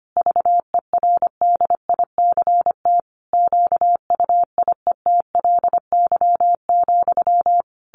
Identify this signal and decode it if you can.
Morse code, heard as a keyed tone, about 25 wpm, 700 Hz. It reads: VERDICT QUIETLY,
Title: A Sound Of Thunder 25wpm